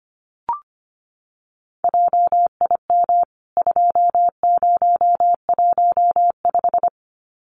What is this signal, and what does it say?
Morse code, heard as a keyed tone, about 25 wpm, 700 Hz. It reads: JIM 2015
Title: CWT Exchange 25wpm